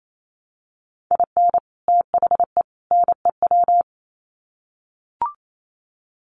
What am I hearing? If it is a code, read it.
Morse code, heard as a keyed tone, about 28 wpm, 700 Hz. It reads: IN THE NEW
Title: Sets of 3 Words 28wpm